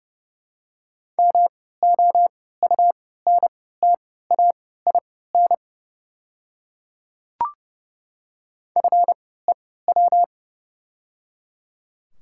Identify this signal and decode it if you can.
Morse code, heard as a keyed tone, about 30 wpm, spaced as for 17 wpm, 700 Hz. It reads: MOUNTAIN FEW
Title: 3x Character Spacing - Top 300 Words - 30wpm